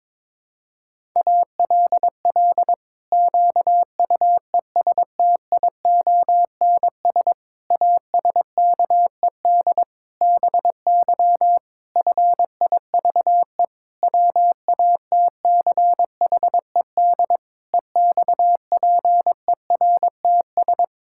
Morse code, heard as a keyed tone, about 22 wpm, 700 Hz. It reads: ALL QUESTIONS ASKED BY FIVE WATCHED EXPERTS
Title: Pangram Challenge v2 - 22wpm